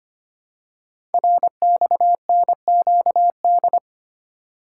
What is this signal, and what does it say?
Morse code, heard as a keyed tone, about 25 wpm, 700 Hz. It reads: RXNQD
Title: Code Groups - 5 Letters - 25wpm